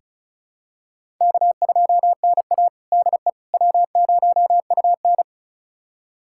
Morse code, heard as a keyed tone, about 36 wpm, 690 Hz. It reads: K2NA DE W0UN
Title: USA Callsign DE Callsign - 4 Characters - 35wpm